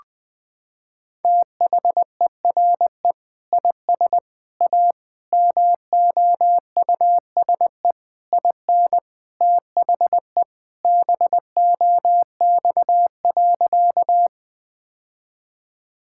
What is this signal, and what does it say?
Morse code, heard as a keyed tone, about 20 wpm, 700 Hz. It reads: THERE IS A MOUSE IN THE BOX.
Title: Sentences from Top 1000 Words - Review Part 5 - 20wpm